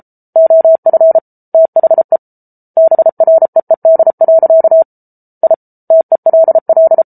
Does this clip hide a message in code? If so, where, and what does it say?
Yes: Morse code from 0.4 to 7.0 s, heard as a keyed tone, about 34 wpm, 650 Hz. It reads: OF THE BREED. I TELL